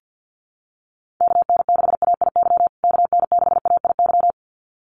Morse code, heard as a keyed tone, about 50 wpm, 700 Hz. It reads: KN6AIY KN6AIY
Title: USA Call Signs Repeated 2x 50wpm